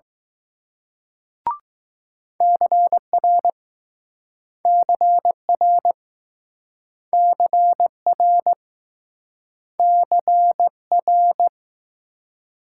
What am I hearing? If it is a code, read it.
Morse code, heard as a keyed tone, about 20 wpm, 700 Hz. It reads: CR CR CR CR